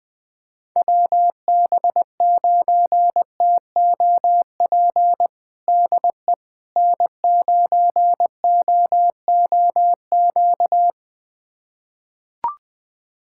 Morse code, heard as a keyed tone, about 20 wpm, 700 Hz. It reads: WB9TOP DE N9OOQ
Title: USA Callsign DE Callsign 20wpm